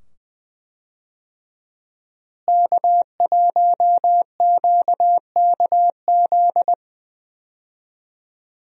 Morse code, heard as a keyed tone, about 20 wpm, 700 Hz. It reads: K1QKZ
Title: USA Call Signs - 5 Characters - Repeated 2x - 20wpm